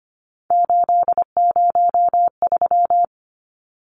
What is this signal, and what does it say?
Morse code, heard as a keyed tone, about 25 wpm, 700 Hz. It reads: 803
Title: Numbers - 3-Digit 25wpm